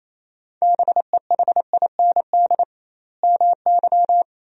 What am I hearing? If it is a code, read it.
Morse code, heard as a keyed tone, about 28 wpm, 700 Hz. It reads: BEHIND MY
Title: 2x Word-Spacing - Sentences from Top 500 Words - 28wpm